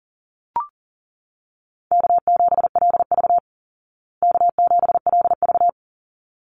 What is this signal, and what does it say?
Morse code, heard as a keyed tone, about 40 wpm, 700 Hz. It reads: K7LV K7LV
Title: Easy Call Signs Repeated 2x - 40wpm